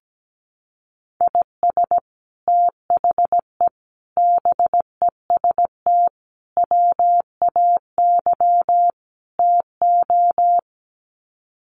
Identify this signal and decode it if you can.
Morse code, heard as a keyed tone, about 17 wpm, 700 Hz. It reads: IS THE BEST WAY TO
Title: Sets of 5 Words Encore 17wpm